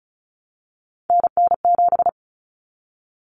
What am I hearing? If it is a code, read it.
Morse code, heard as a keyed tone, about 34 wpm, 700 Hz. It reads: NN7
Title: Common Cut Numbers - 3-Digit 35wpm